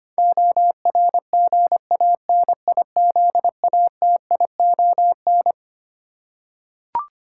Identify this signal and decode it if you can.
Morse code, heard as a keyed tone, about 25 wpm, 700 Hz. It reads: ORGANIZATION
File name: Twelve Letter Words - 25wpm